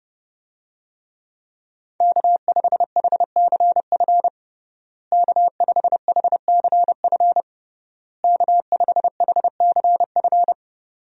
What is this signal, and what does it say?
Morse code, heard as a keyed tone, about 32 wpm, 700 Hz. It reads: K5HCF K5HCF K5HCF
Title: USA Call Signs - 5 Characters - Repeated 3x - 30wpm